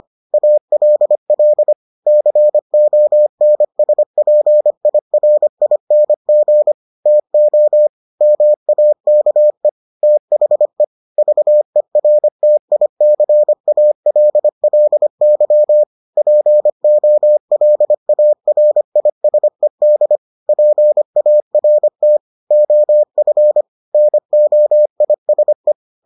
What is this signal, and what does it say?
Morse code, heard as a keyed tone, about 25 wpm, 600 Hz. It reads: ALL CONSPIRING TO MAKE THE VERTICALLY POLARISED PART OF NOISE